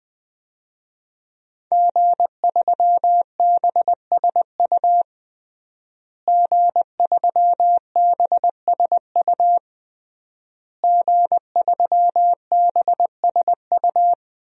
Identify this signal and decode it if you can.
Morse code, heard as a keyed tone, about 20 wpm, 700 Hz. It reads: G3BSU G3BSU G3BSU
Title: DXCC-100 Call Signs - 5 Characters - Repeated 3x - 20wpm